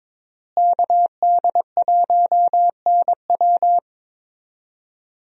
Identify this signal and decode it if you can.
Morse code, heard as a keyed tone, about 22 wpm, 700 Hz. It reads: KD1NW
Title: USA Call Signs - 5 Characters - Repeated 3x - 22wpm